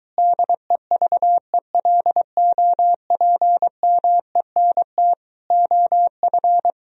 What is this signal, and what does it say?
Morse code, heard as a keyed tone, about 23 wpm, 700 Hz. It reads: DEVELOPMENT OF